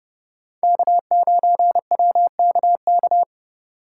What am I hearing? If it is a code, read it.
Morse code, heard as a keyed tone, about 30 wpm, 700 Hz. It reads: K9WKK